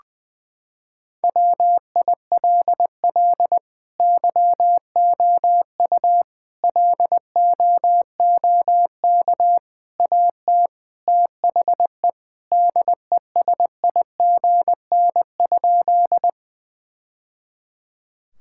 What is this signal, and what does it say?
Morse code, heard as a keyed tone, about 19 wpm, 700 Hz. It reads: WILL YOU LOOK AT THE DESIGN?
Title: Sentences from Top 1000 Words - Review Part 5 - 20wpm